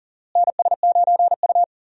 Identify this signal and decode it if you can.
Morse code, heard as a keyed tone, about 40 wpm, 700 Hz. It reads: NS9U
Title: USA Call Signs - 4 Characters - Repeated 2x - 40wpm